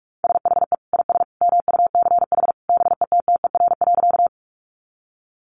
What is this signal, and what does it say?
Morse code, heard as a keyed tone, about 44 wpm, 700 Hz. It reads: SHE IS MUCH BETTER.
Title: Sentences from Top 400 Words 45wpm